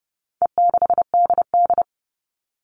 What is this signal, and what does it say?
Morse code, heard as a keyed tone, about 30 wpm, 700 Hz. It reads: E6DD
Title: All Cut Numbers - 4-Digit 30wpm